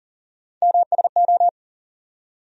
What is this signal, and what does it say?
Morse code, heard as a keyed tone, about 40 wpm, 700 Hz. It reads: MSO
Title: USA Call Sign Postfixes - 3 Letters 40wpm